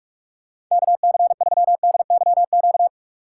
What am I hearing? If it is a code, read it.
Morse code, heard as a keyed tone, about 44 wpm, 700 Hz. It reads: KC3DYQ
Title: USA Call Signs - 3x Character Spacing - 45wpm